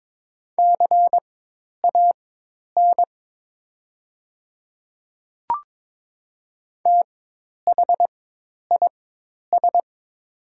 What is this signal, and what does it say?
Morse code, heard as a keyed tone, about 22 wpm, spaced as for 10 wpm, 690 Hz. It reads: CAN THIS
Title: 4x Character Spacing - Top 500 Words - 22wpm